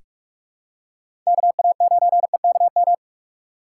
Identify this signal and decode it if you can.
Morse code, heard as a keyed tone, about 42 wpm, 700 Hz. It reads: K A 9 E K M
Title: USA Call Signs - 6 Characters - 6x Character Spacing - 45wpm